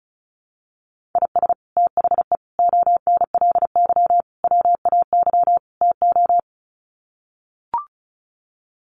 Morse code, heard as a keyed tone, about 34 wpm, 700 Hz. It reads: IS THE ONLY WAY TO
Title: Sets of 5 Words Encore 35wpm